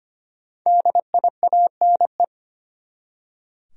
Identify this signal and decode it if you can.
Morse code, heard as a keyed tone, about 25 wpm, 700 Hz. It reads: DIANE
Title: USA Common Names - Max 5 Letters - 25wpm